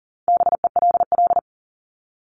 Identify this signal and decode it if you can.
Morse code, heard as a keyed tone, about 40 wpm, 700 Hz. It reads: BELL
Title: Top 1751-2000 Words 40wpm